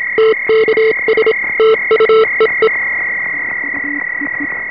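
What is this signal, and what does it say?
Morse code, heard as a keyed tone, about 27 wpm, 440 Hz, with noise. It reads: TKS TU EE
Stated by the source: 3.5 MHz CW received on an experimental direct digital HF receiver with RF band pass filters for the 3.5 and 7 MHz ham bands.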